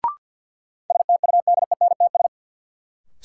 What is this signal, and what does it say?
Morse code, heard as a keyed tone, about 52 wpm, 700 Hz. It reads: STUDENTS
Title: Sets of 5 Words Encore - New Word Drill - 50wpm